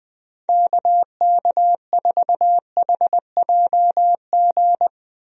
Morse code, heard as a keyed tone, about 21 wpm, 700 Hz. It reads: KK4HJG
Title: USA Call Signs - 6 Characters - Repeated 3x - 20wpm